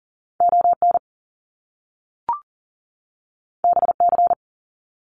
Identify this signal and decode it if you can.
Morse code, heard as a keyed tone, about 41 wpm, 700 Hz. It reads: ON BC
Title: Canadian Provinces and Territories 40wpm